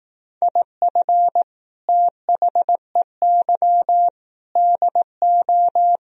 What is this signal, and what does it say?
Morse code, heard as a keyed tone, about 18 wpm, 700 Hz. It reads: IF THEY DO
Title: Speed-Racing - Sets of 4 Words 15wpm